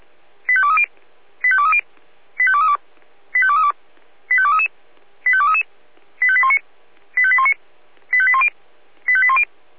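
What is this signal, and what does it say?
Five-tone selective call (ZVEI), heard as quick runs of five tones, tones of about 70 ms. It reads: 85230 85230 85232 85232 85233 85233 85819 85819 85810 85810
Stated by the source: ZVEI five-tone selective call received on SDR.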